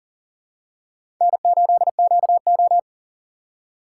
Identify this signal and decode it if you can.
Morse code, heard as a keyed tone, about 40 wpm, 700 Hz. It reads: N8QO
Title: USA Call Signs - 4x Character Spacing - 40wpm